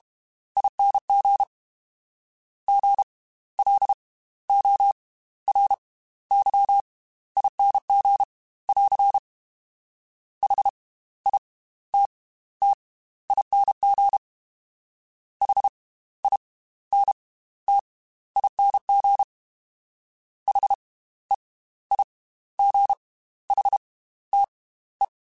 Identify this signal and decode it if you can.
Morse code, heard as a keyed tone, about 32 wpm, spaced as for 15 wpm, 800 Hz. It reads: ING GLORYING <AR> HITTING HINTING HEIGHTE